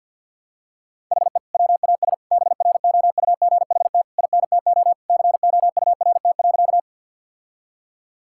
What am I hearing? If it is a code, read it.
Morse code, heard as a keyed tone, about 51 wpm, 700 Hz. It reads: HE WAS BROUGHT INTO COURT.
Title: Sentences from Top 1600 Words 50wpm